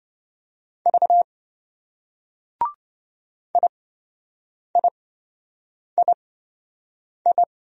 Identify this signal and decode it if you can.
Morse code, heard as a keyed tone, about 26 wpm, spaced as for 7 wpm, 700 Hz. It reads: V IIII